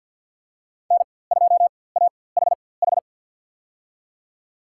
Morse code, heard as a keyed tone, about 53 wpm, spaced as for 24 wpm, 700 Hz. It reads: N2AHH
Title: USA Call Signs - 5 Characters - 4x Character Spacing - 50wpm